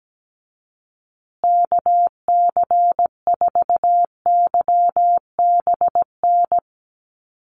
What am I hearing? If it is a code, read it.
Morse code, heard as a keyed tone, about 17 wpm, 700 Hz. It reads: KC4YBN
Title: USA Call Signs Repeated 3x 17wpm